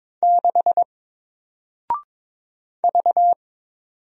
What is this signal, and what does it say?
Morse code, heard as a keyed tone, about 22 wpm, 700 Hz. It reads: 6 V
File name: Single-Letter-Number-22wpm